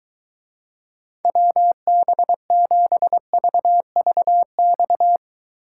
Morse code, heard as a keyed tone, about 23 wpm, 700 Hz. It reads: WB7VVX